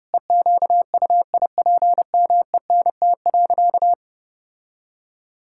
Morse code, heard as a keyed tone, about 32 wpm, 700 Hz. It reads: EQUIPMENT.
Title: Sentences from Top 1000 Words - Review Part 5 - 30wpm